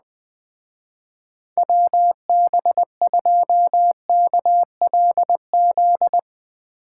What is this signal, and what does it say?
Morse code, heard as a keyed tone, about 20 wpm, 700 Hz. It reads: WB2KLZ
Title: Speed Racing - USA 2x3 Call Signs 17wpm